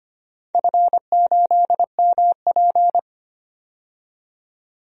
Morse code, heard as a keyed tone, about 25 wpm, 700 Hz. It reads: F8MP
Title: DXCC-100 Call Signs - 4 Characters - 25wpm